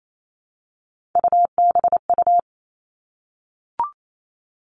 Morse code, heard as a keyed tone, about 28 wpm, 700 Hz. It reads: UBU
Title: Three Challenging Letter-Number Combinations 28wpm